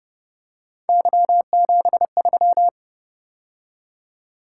Morse code, heard as a keyed tone, about 30 wpm, 700 Hz. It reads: Y73
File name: Speed-Racing - Three Letter-Number Combinations - 25wpm